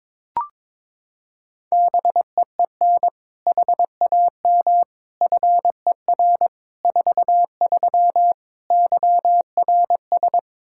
Morse code, heard as a keyed tone, about 22 wpm, 700 Hz. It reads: BEEN HAM FER 43 YRS
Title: Practice QSO - Exchange 3 - More Info - 22wpm